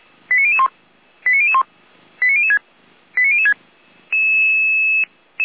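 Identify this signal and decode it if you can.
Five-tone selective call (ZVEI), heard as quick runs of five tones, tones of about 70 ms. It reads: 89001 89001 89006 89006
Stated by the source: ZVEI five-tone call 89033 received on SDR, from the audio server of Feuerwehr Orlamünde.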